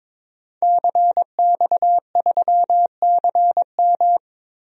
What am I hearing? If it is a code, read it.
Morse code, heard as a keyed tone, about 21 wpm, 700 Hz. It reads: CX3CM
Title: DXCC-100 Call Signs - 5 Characters - Repeated 3x - 22wpm